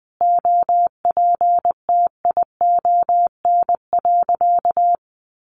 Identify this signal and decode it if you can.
Morse code, heard as a keyed tone, about 20 wpm, 700 Hz. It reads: OPTION.
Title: Sentences from Top 1300 Words 20wpm